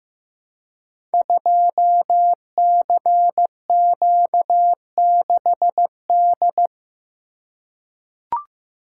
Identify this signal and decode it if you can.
Morse code, heard as a keyed tone, about 15 wpm, 700 Hz. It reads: 2CQ6D
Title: Code Groups - 5 Letter:Numbers - 15wpm